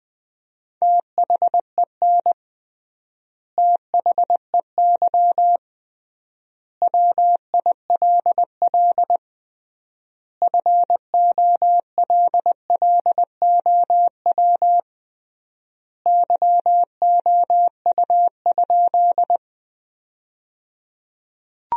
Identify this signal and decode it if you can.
Morse code, heard as a keyed tone, about 20 wpm, 700 Hz. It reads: THEN THEY WILL FOLLOW YOU?
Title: 3x Word-Spacing - Sentences from Top 200 Words - 20wpm